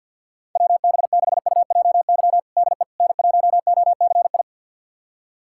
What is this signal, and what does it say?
Morse code, heard as a keyed tone, about 48 wpm, 700 Hz. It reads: WB6UJY DE N1OCI
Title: USA Callsign DE Callsign 50wpm